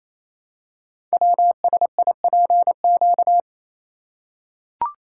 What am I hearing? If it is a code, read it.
Morse code, heard as a keyed tone, about 29 wpm, 700 Hz. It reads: WSIPQ
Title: Code Groups - 5 Letters - 28wpm